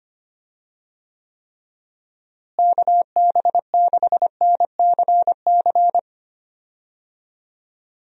Morse code, heard as a keyed tone, about 25 wpm, 700 Hz. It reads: KB6NCC
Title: USA Call Signs - 6 Characters - Repeated 2x - 25wpm